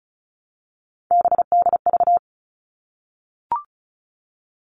Morse code, heard as a keyed tone, about 34 wpm, 700 Hz. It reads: BDV
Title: All Cut Numbers - 3-Digit 35wpm